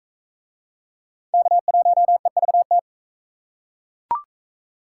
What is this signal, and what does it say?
Morse code, heard as a keyed tone, about 37 wpm, 700 Hz. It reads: K1EVT